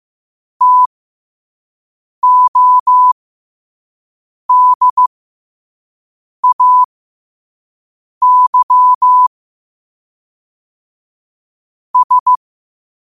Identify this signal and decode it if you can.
Morse code, heard as a keyed tone, about 15 wpm, spaced as for 5 wpm, 1000 Hz. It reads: TODAY S
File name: 5wpm